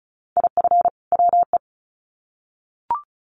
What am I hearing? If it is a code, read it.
Morse code, heard as a keyed tone, about 34 wpm, 700 Hz. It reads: IF WE
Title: Sets of 2 Words 35wpm